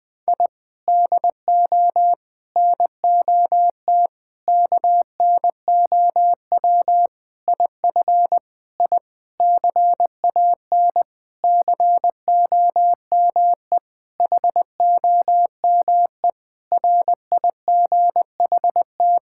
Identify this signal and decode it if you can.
Morse code, heard as a keyed tone, about 20 wpm, 700 Hz. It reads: I DO NOT KNOW IF I CAN COME HOME RIGHT
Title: Sentences with 5-Word Phrases - Spoken Beginning - 20wpm